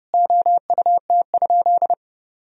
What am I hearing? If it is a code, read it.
Morse code, heard as a keyed tone, about 30 wpm, 700 Hz. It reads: OUT?
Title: Sentences from Top 200 Words and Abbreviations 30wpm